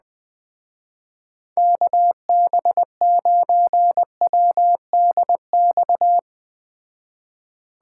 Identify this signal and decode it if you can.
Morse code, heard as a keyed tone, about 20 wpm, 700 Hz. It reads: KB9WDX